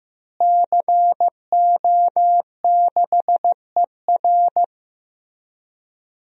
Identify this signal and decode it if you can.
Morse code, heard as a keyed tone, about 15 wpm, 700 Hz. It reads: CO6ER
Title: DXCC-100 Call Signs - 5 Characters - Repeated 3x - 15wpm